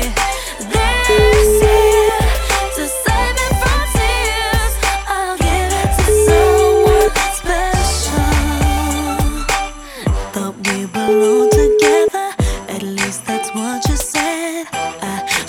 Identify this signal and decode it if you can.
Ringback tone (Europe, 425 Hz), with noise.